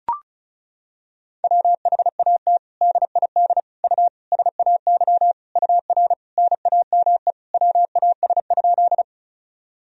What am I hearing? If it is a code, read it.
Morse code, heard as a keyed tone, about 34 wpm, 700 Hz. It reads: WHAT DID U SAY UR NAME WAS?
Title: Sentences from Top 200 Words and Abbreviations 35wpm